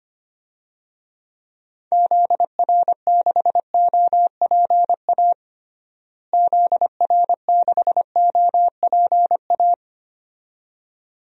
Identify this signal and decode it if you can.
Morse code, heard as a keyed tone, about 25 wpm, 700 Hz. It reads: ZR6OPA ZR6OPA
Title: DXCC-100 Call Signs - 6 Characters - Repeated 2x - 25wpm